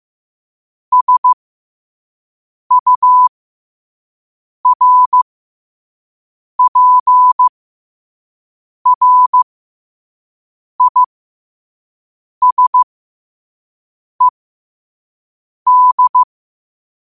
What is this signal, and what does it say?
Morse code, heard as a keyed tone, about 15 wpm, spaced as for 5 wpm, 1000 Hz. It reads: SURPRISED
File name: qotdfiles / 5wpm